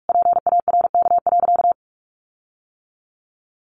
Morse code, heard as a keyed tone, about 44 wpm, 700 Hz. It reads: PARK.
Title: Sentences from Top 1300 Words 45wpm